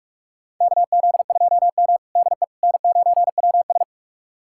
Morse code, heard as a keyed tone, about 46 wpm, 700 Hz. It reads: KZ2M DE N9WS
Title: USA Callsign DE Callsign - 4 Characters - 45wpm